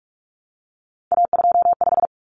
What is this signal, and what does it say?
Morse code, heard as a keyed tone, about 44 wpm, 700 Hz. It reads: A25
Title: Common Cut Numbers - 3-Digit 45wpm